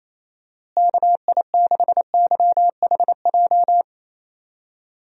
Morse code, heard as a keyed tone, about 27 wpm, 700 Hz. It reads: KI6YHJ
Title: USA Call Signs - 6 Characters - Repeated 2x - 28wpm